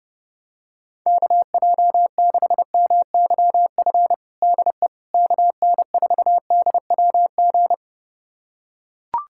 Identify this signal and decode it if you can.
Morse code, heard as a keyed tone, about 30 wpm, 700 Hz. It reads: KJ6MYF DE KN4DWG
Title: USA Callsign DE Callsign 30wpm